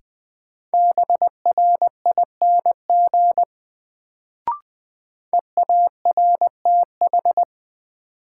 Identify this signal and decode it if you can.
Morse code, heard as a keyed tone, about 20 wpm, 700 Hz. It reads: BRING EARTH
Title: Sets of 5 Words Encore - New Word Drill - 20wpm